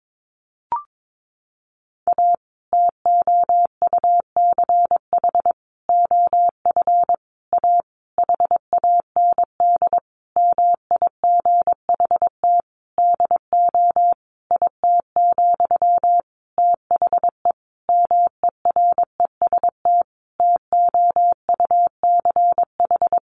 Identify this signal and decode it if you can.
Morse code, heard as a keyed tone, about 22 wpm, 700 Hz. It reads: A TOUCH OF A HAND MIGHT DO IT, THE MEREST TOUCH
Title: A Sound Of Thunder 22wpm